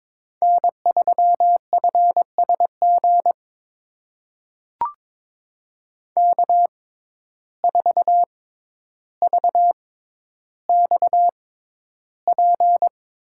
Morse code, heard as a keyed tone, about 23 wpm, spaced as for 11 wpm, 700 Hz. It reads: N3FSG K4VXP
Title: USA Call Signs - 5 Characters - 6x Character Spacing - 22wpm